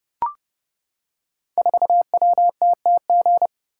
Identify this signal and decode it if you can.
Morse code, heard as a keyed tone, about 28 wpm, 700 Hz. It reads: 4WTTG
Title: Code Groups - 5 Letter:Numbers - 30wpm